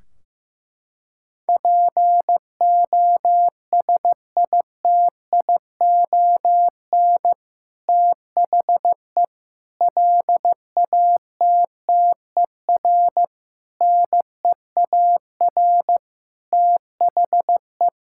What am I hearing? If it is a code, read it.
Morse code, heard as a keyed tone, about 15 wpm, 700 Hz. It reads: POSITION THE LATTER NEAR THE
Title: Sentences from Top 1600 Words 15wpm